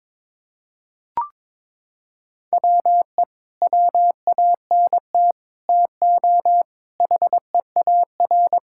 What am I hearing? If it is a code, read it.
Morse code, heard as a keyed tone, about 22 wpm, 700 Hz. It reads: WE WANT TO HEAR
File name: practice / 22wpm / Sets of 8 Words - 22wpm